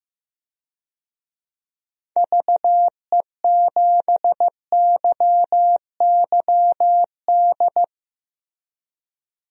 Morse code, heard as a keyed tone, about 15 wpm, 700 Hz. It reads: VE7YYD
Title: DXCC-100 Call Signs - 6 Characters - Repeated 2x - 15wpm